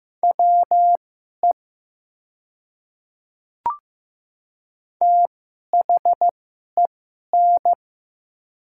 Morse code, heard as a keyed tone, about 15 wpm, spaced as for 11 wpm, 700 Hz. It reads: WE THEN
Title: 2x Character Spacing - Top 100 Words - 15wpm